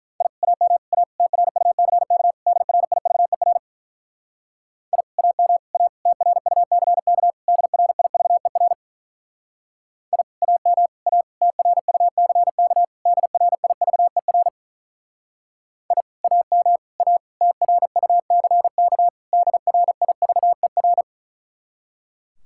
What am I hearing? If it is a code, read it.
Morse code, heard as a keyed tone, about 48 wpm, 700 Hz. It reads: I AM A TRUCK DRIVER I AM A TRUCK DRIVER I AM A TRUCK DRIVER I AM A TRUCK DRIVER